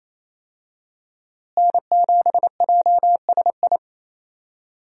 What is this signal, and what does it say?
Morse code, heard as a keyed tone, about 28 wpm, 700 Hz. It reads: N7JSI
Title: USA Call Signs - 5 Characters - Repeated 2x - 28wpm